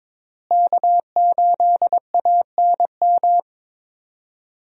Morse code, heard as a keyed tone, about 23 wpm, 700 Hz. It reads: K8ANM
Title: USA Call Signs - 5 Characters - Repeated 3x - 22wpm